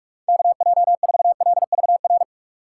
Morse code, heard as a keyed tone, about 41 wpm, 700 Hz. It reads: KJ4LVR
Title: USA Call Signs - 6 Characters - 3x Character Spacing - 45wpm